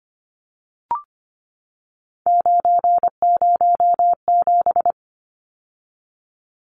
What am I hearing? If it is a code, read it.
Morse code, heard as a keyed tone, about 25 wpm, 700 Hz. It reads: 907
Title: Numbers - 3-Digit 25wpm